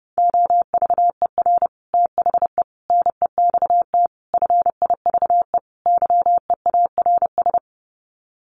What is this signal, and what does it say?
Morse code, heard as a keyed tone, about 30 wpm, 700 Hz. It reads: OVER THE NEXT FIVE YEARS
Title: Sets of 5 Words 30wpm